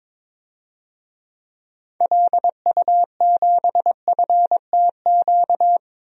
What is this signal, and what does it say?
Morse code, heard as a keyed tone, about 21 wpm, 700 Hz. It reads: LU7FTQ
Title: DXCC-100 Call Signs - 6 Characters - 22wpm